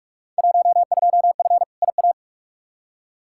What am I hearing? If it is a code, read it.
Morse code, heard as a keyed tone, about 46 wpm, 700 Hz. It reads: 12F IA
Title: Field Day Exchanges Encore - 45wpm